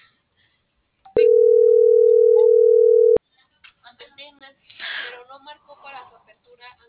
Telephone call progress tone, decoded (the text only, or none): ringback tone